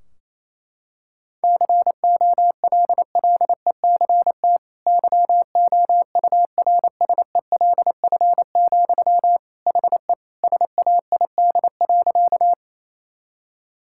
Morse code, heard as a keyed tone, about 28 wpm, 700 Hz. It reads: COLLECT YOURSELF, HE SAID.
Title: Sentences from Top 1600 Words 28wpm